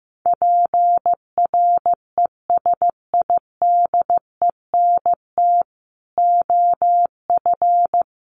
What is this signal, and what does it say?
Morse code, heard as a keyed tone, about 15 wpm, 700 Hz. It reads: PRESIDENT OF
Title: Sets of 3 Words 15wpm